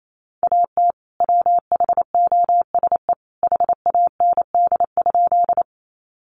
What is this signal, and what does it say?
Morse code, heard as a keyed tone, about 28 wpm, 700 Hz. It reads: AT WHOSE HAND?
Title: Sentences from Top 1200 Words 28wpm